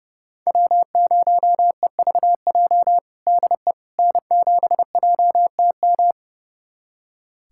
Morse code, heard as a keyed tone, about 30 wpm, 700 Hz. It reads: W0EVJ DE N7JTM
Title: USA Callsign DE Callsign - 5 Characters - 30wpm